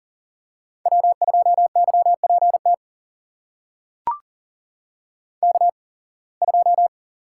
Morse code, heard as a keyed tone, about 40 wpm, spaced as for 26 wpm, 700 Hz. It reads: W2YPT K2
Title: USA Call Signs - 5 Characters - 8x Character Spacing - 40wpm